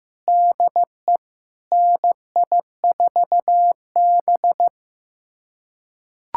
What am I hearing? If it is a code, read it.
Morse code, heard as a keyed tone, about 16 wpm, 700 Hz. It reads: DE NI4B
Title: USA Callsign DE Callsign - 4 Characters - 15wpm